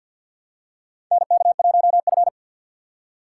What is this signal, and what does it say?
Morse code, heard as a keyed tone, about 46 wpm, 700 Hz. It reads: NK1F
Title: USA Call Signs - 4 Characters - 2x Character Spacing - 50wpm